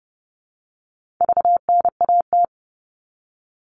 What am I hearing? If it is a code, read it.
Morse code, heard as a keyed tone, about 30 wpm, 700 Hz. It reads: VNAT
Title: All Cut Numbers - 4-Digit 30wpm